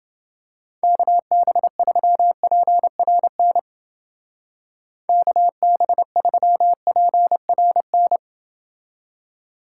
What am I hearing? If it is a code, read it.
Morse code, heard as a keyed tone, about 30 wpm, 700 Hz. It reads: KB3PRN KB3PRN